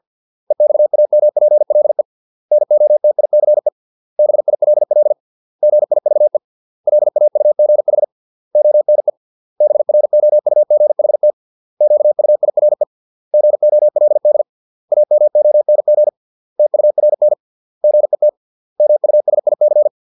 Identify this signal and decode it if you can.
Morse code, heard as a keyed tone, about 50 wpm, 600 Hz. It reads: EXAMPLE NOTICE BILL GIVE LAUGH ONE BROUGHT QUIRE GOLD AMONG TURN GET MUSIC
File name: 632 words / 632 Words 50WPM